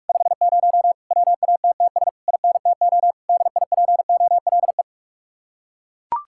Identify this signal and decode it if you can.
Morse code, heard as a keyed tone, about 49 wpm, 700 Hz. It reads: 50 WATTS INTO DIPOLE